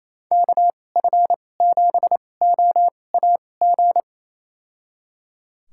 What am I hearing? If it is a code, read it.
Morse code, heard as a keyed tone, about 28 wpm, spaced as for 20 wpm, 700 Hz. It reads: KF7OAG
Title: USA Call Signs - 6 Characters - 2x Character Spacing - 28wpm